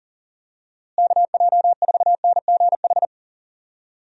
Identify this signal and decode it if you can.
Morse code, heard as a keyed tone, about 37 wpm, 700 Hz. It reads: KJ4NGH
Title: USA Call Signs - 6 Characters - Repeated 2x - 40wpm